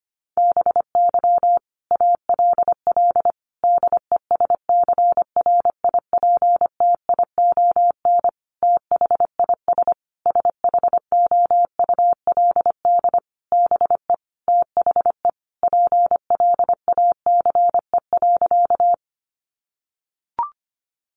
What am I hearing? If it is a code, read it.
Morse code, heard as a keyed tone, about 25 wpm, 700 Hz. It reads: BY ALL DESCRIPTION THIS SHOULD BE THE PLACE.
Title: Sentences from Top 1300 Words 25wpm